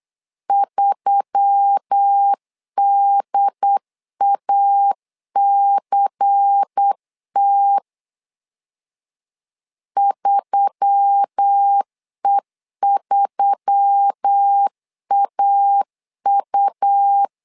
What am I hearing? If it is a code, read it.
Morse code, heard as a keyed tone, about 8 wpm, 800 Hz. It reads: 3DACT 3E3AU
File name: LEZIONI / 5 LBVC / 5 LBVC 8wpm 2935